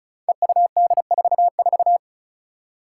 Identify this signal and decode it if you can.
Morse code, heard as a keyed tone, about 36 wpm, 700 Hz. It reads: EUD44
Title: Five Challenging Letter-Number Combinations 35wpm